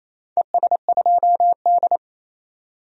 Morse code, heard as a keyed tone, about 28 wpm, 700 Hz. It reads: ES2D
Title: DXCC-100 Call Signs - 4 Characters - 28wpm